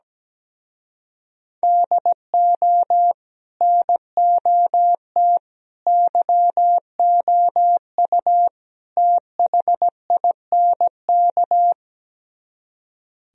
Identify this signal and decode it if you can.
Morse code, heard as a keyed tone, about 17 wpm, 700 Hz. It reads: DO NOT YOU THINK
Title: Speed-Racing - Sets of 4 Words 17wpm